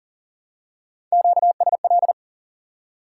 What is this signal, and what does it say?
Morse code, heard as a keyed tone, about 40 wpm, 700 Hz. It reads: QSL
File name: IQE Course 3 - Lesson 89 - YAGI - Two Elements - 40wpm